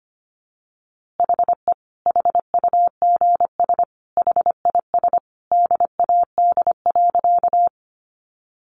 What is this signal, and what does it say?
Morse code, heard as a keyed tone, about 25 wpm, 700 Hz. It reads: HE HUGS HIS DAD.
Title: Easy Phrases 25wpm